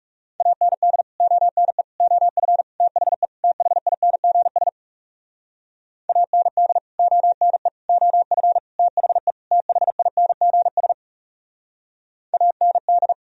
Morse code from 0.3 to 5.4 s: AND ONE OF THE THINGS

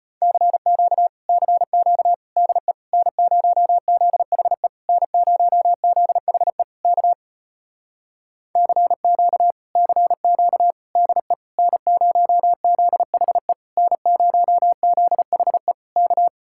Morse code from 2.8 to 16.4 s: N0ZHE N0ZHE K CQ CQ DE N0ZHE N0ZHE K

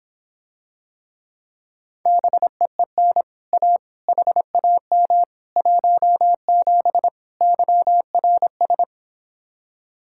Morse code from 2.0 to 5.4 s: BEEN A HAM

